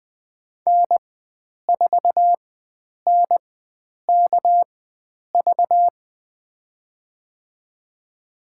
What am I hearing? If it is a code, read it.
Morse code, heard as a keyed tone, about 20 wpm, spaced as for 9 wpm, 700 Hz. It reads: N4NKV
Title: USA Call Signs - 5 Characters - 4x Character Spacing - 20wpm